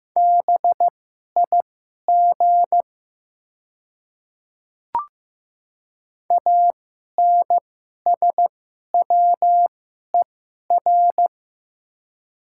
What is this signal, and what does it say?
Morse code, heard as a keyed tone, about 15 wpm, spaced as for 11 wpm, 700 Hz. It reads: BIG ANSWER
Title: 2x Character Spacing - Top 300 Words - 15wpm